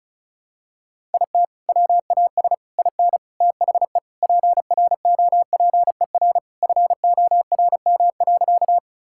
Morse code, heard as a keyed tone, about 34 wpm, 700 Hz. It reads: IT WAS IN THE PROPER FORM.
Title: Sentences from Top 1000 Words - Review Part 5 - 35wpm